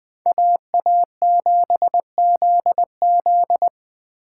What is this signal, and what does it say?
Morse code, heard as a keyed tone, about 21 wpm, 700 Hz. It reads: AA7ZZ
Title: Speed Racing - USA 2x2 Call Signs 17wpm